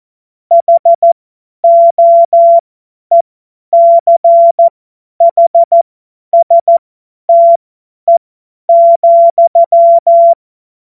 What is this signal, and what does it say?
Morse code, heard as a keyed tone, about 14 wpm, spaced as for 10 wpm, 680 Hz. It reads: HOECHSTE,
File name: Morsekurs / Witze / Witz 05 März 2024 50 F70